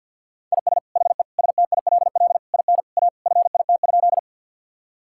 Morse code, heard as a keyed tone, about 51 wpm, 700 Hz. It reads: IS HE STILL IN A FIT?
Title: Sentences from Top 1000 Words - Review Part 5 - 50wpm